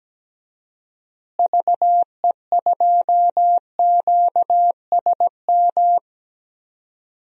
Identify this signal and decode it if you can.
Morse code, heard as a keyed tone, about 17 wpm, 700 Hz. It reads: VE2QSM
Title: DXCC-100 Call Signs - 6 Characters - Repeated 2x - 17wpm